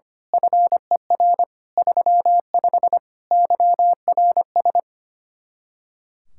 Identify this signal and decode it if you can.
Morse code, heard as a keyed tone, about 24 wpm, 700 Hz. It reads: FER 35 YRS
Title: Speed-Racing Common QSO Phrases - 25wpm